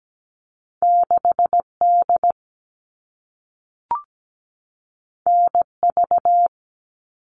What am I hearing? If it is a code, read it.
Morse code, heard as a keyed tone, about 17 wpm, 700 Hz. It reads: 6D NV
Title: All Cut Numbers - 2-Digit 17wpm